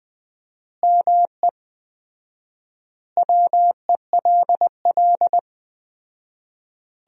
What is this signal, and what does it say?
Morse code, heard as a keyed tone, about 20 wpm, 700 Hz. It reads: ME WELL
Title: 4x Word-Spacing - Sentences from Top 400 Words - 20wpm